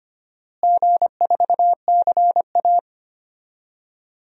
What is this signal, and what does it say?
Morse code, heard as a keyed tone, about 25 wpm, 700 Hz. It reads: G4CA
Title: DXCC-100 Call Signs - 4 Characters - Repeated 3x - 25wpm